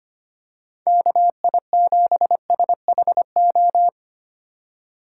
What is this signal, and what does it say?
Morse code, heard as a keyed tone, about 25 wpm, 700 Hz. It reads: KI7SHO
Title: USA Call Signs - 6 Characters - 3x Character Spacing - 25wpm